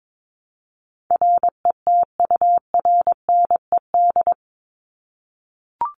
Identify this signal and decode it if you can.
Morse code, heard as a keyed tone, about 22 wpm, 700 Hz. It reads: RETURNED
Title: ED Suffix - Top 100 Words with Base Words 22wpm